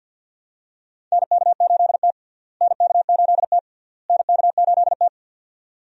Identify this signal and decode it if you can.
Morse code, heard as a keyed tone, about 53 wpm, 700 Hz. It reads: NK8T NK8T NK8T
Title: USA Call Signs - 4 Characters - Repeated 3x - 50wpm